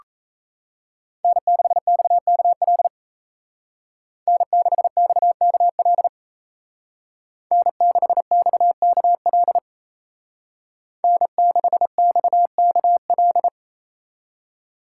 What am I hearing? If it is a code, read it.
Morse code, heard as a keyed tone, about 35 wpm, 700 Hz. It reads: N6XKL N6XKL N6XKL N6XKL